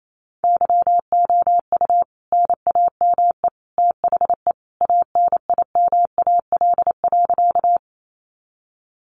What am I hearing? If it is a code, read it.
Morse code, heard as a keyed tone, about 28 wpm, 700 Hz. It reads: YOU NAME THE ANIMAL.
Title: A Sound Of Thunder 28wpm